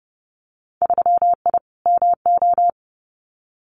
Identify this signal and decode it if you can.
Morse code, heard as a keyed tone, about 29 wpm, 700 Hz. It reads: 3I MO
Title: Winter Field Day Exchanges 30wpm